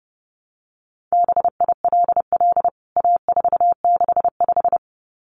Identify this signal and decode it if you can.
Morse code, heard as a keyed tone, about 28 wpm, 700 Hz. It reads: BILL A465
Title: CWT Exchange 30wpm